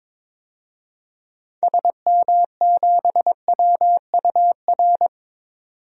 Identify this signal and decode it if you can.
Morse code, heard as a keyed tone, about 22 wpm, 700 Hz. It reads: SM7WUR
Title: DXCC-100 Call Signs - 6 Characters - 22wpm